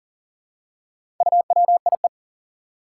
Morse code, heard as a keyed tone, about 40 wpm, 700 Hz. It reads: UWIE
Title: Four Challenging Letter-Number Combinations 40wpm